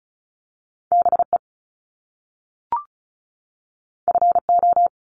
Morse code, heard as a keyed tone, about 36 wpm, 700 Hz. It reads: BE FO
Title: Top Two Letter Combinations in Words 35wpm